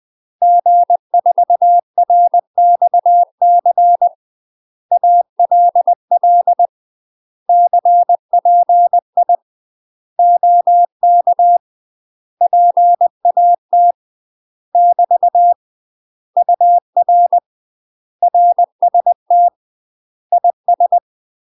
Morse code, heard as a keyed tone, about 20 wpm, 700 Hz. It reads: G4RXC ALL CPI OK PAT <BT> UR RST IS
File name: UK MASTER QSOs TRAINING / 20WPM / UK MASTER QSOs 0011